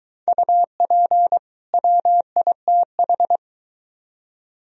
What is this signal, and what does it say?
Morse code, heard as a keyed tone, about 22 wpm, 700 Hz. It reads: UP WITH